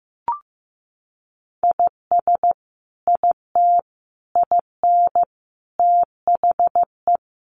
Morse code, heard as a keyed tone, about 15 wpm, 700 Hz. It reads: IS IT IN THE
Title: Sentences from Top 1200 Words 15wpm